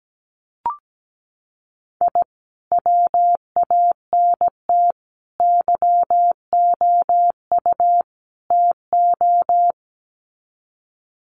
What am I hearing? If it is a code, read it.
Morse code, heard as a keyed tone, about 17 wpm, 700 Hz. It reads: I WANT YOU TO
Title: Sets of 4 Words 17wpm